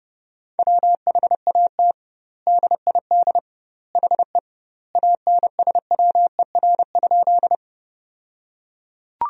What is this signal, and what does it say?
Morse code, heard as a keyed tone, about 30 wpm, 700 Hz. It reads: WHAT DID HE ANSWER?
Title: 2x Word-Spacing - Sentences from Top 300 Words - 30wpm